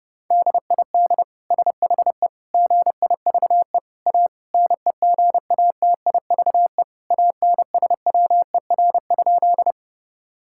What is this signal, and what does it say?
Morse code, heard as a keyed tone, about 30 wpm, 700 Hz. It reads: DID SHE GIVE A NEGATIVE ANSWER?
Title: Sentences from Top 1000 Words - Review Part 6 - 30wpm